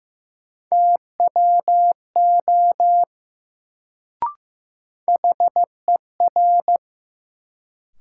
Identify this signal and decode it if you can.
Morse code, heard as a keyed tone, about 15 wpm, 700 Hz. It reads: TWO HER
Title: Sentences from Top 100 Words - New Word Drill - 15wpm